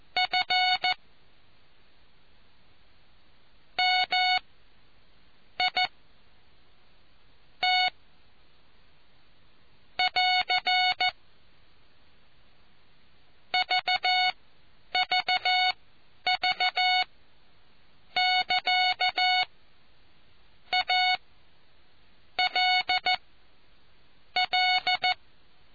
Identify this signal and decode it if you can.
Morse code, heard as a keyed tone, about 14 wpm, spaced as for 9 wpm, 740 Hz. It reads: F M I T <AR> VVV <KA> A L L